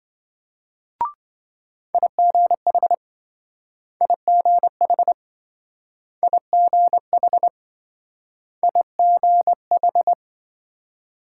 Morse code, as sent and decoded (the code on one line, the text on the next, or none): .. --. .... / .. --. .... / .. --. .... / .. --. ....
IGH IGH IGH IGH